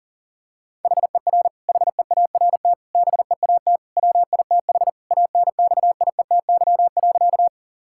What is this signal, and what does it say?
Morse code, heard as a keyed tone, about 40 wpm, 700 Hz. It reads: HER HEART BEAT WITH ANXIETY.
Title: Sentences from Top 1600 Words 40wpm